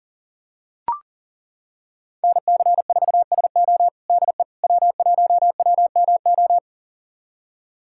Morse code, heard as a keyed tone, about 40 wpm, 700 Hz. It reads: NC4SO DE W1WMO
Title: USA Callsign DE Callsign - 5 Characters - 40wpm